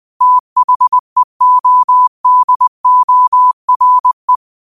Morse code, heard as a keyed tone, about 20 wpm, 1000 Hz. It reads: THEODORE